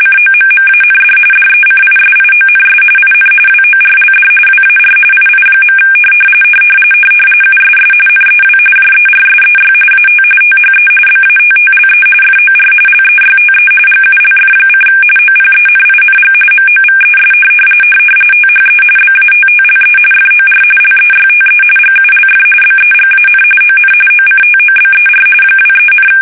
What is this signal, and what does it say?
RTTY, about 45.45 baud. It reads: COMMAND DECISIONS IN WORLD WAR II. USS NEW JERSEY AT THE BATTLE FOR LEYTE GULF: OPENING MOVES THE BATTLE FOR LEYTE GULF BEGAN ON OCTOBER 23,
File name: fsk850 / battleshipnj